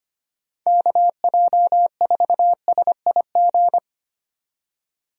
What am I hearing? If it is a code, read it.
Morse code, heard as a keyed tone, about 25 wpm, 700 Hz. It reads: KJ4SIG
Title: USA Call Signs - 6 Characters - 25wpm